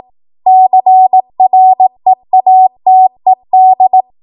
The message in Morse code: CREATED

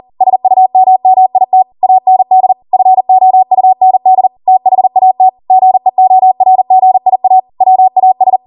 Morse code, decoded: SUMMIT AND FOUND THAT GEORGIA WAS

